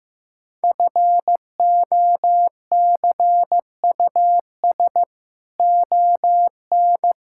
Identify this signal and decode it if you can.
Morse code, heard as a keyed tone, about 16 wpm, 700 Hz. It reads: FOCUS ON
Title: Sentences from Top 1000 Words - Review Part 6 - 15wpm